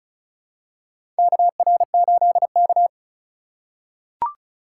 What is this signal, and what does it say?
Morse code, heard as a keyed tone, about 36 wpm, 700 Hz. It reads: KR8K
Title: USA Call Signs - 4 Characters - 2x Character Spacing - 35wpm